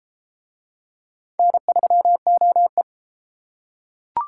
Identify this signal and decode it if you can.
Morse code, heard as a keyed tone, about 31 wpm, 700 Hz. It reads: N3OE